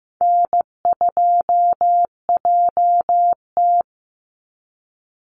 Morse code, heard as a keyed tone, about 15 wpm, 700 Hz. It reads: N2JT
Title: Easy Call Signs - 15wpm